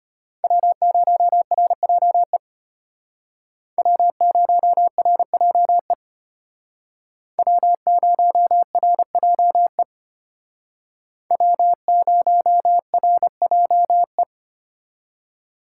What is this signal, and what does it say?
Morse code, heard as a keyed tone, about 32 wpm, 700 Hz. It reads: W0RJE W0RJE W0RJE W0RJE